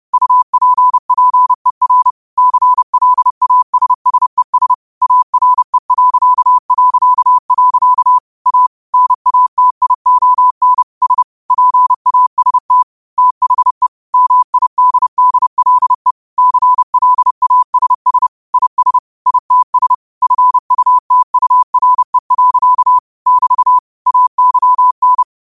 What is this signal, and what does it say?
Morse code, heard as a keyed tone, about 30 wpm, 1000 Hz. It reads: UPPER CLASSES ARE... A NATION S PAST THE MIDDLE CLASS IS ITS FUTURE. <BT> AYN